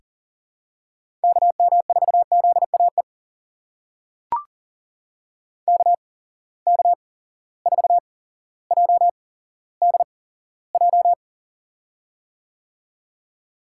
Morse code, heard as a keyed tone, about 38 wpm, spaced as for 17 wpm, 700 Hz. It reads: KM4ZAE KK4JDJ
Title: USA Call Signs - 8x Character Spacing - 40wpm